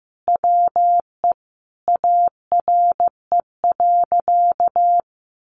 Morse code, heard as a keyed tone, about 14 wpm, 700 Hz. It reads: WE ARE.
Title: Sentences from Top 1400 Words 15wpm